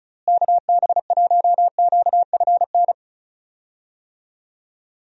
Morse code, heard as a keyed tone, about 36 wpm, 700 Hz. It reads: KB1QFN
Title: USA Call Signs - 6 Characters - Repeated 2x - 35wpm